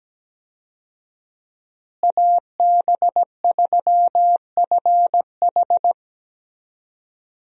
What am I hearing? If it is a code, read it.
Morse code, heard as a keyed tone, about 17 wpm, 700 Hz. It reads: AB3FH
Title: USA Call Signs - 5 Characters - Repeated 2x - 17wpm